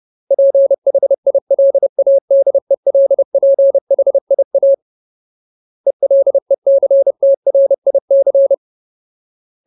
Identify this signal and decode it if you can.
Morse code, heard as a keyed tone, about 30 wpm, 550 Hz. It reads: PHILADELPHIA ELECTRIC
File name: Ultimate Word List / 30WPM Ultimate